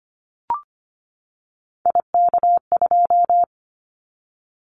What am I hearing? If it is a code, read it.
Morse code, heard as a keyed tone, about 25 wpm, 700 Hz. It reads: IK2
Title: Callsign Prefixes for Top 25 DXCC - 25wpm